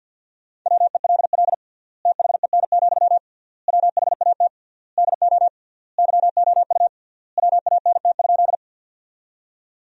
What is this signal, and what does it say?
Morse code, heard as a keyed tone, about 50 wpm, 700 Hz. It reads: WELL THEN, WHAT DO YOU WANT?
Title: 3x Word-Spacing - Sentences from Top 200 Words - 50wpm